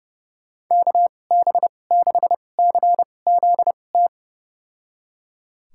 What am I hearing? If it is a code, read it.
Morse code, heard as a keyed tone, about 30 wpm, spaced as for 22 wpm, 700 Hz. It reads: KB6CZT
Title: USA Call Signs - 6 Characters - 2x Character Spacing - 30wpm